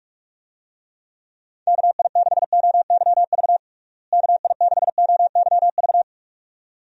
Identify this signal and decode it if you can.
Morse code, heard as a keyed tone, about 43 wpm, 700 Hz. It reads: KI6OYV KI6OYV
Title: USA Call Signs - 6 Characters - Repeated 2x - 45wpm